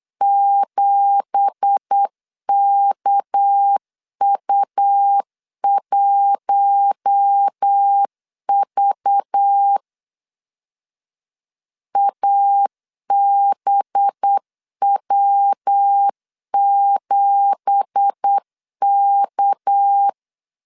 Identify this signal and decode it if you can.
Morse code, heard as a keyed tone, about 8 wpm, 800 Hz. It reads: 7KU1V ABW7K
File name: LEZIONI / 9 2867 8wpm 3440